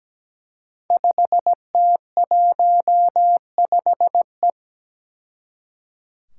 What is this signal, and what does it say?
Morse code, heard as a keyed tone, about 17 wpm, 700 Hz. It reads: 5T15E
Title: Five Challenging Letter-Number Combinations 17wpm